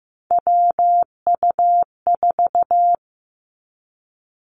Morse code, heard as a keyed tone, about 15 wpm, 700 Hz. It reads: WU4
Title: Three Challenging Letter-Number Combinations 15wpm